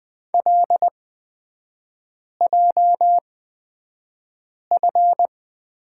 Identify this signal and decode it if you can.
Morse code, heard as a keyed tone, about 20 wpm, spaced as for 5 wpm, 700 Hz. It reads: LJF
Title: 224 Lesson 36e - BT - Three Characters 20wpm